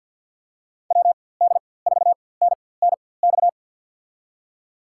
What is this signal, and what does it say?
Morse code, heard as a keyed tone, about 53 wpm, spaced as for 24 wpm, 700 Hz. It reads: WD4NNX
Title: USA Call Signs - 4x Character Spacing - 50wpm